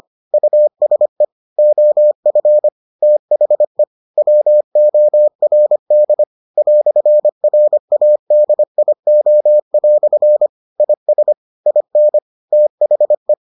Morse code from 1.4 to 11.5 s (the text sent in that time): OF THE WORD "RADIO" IS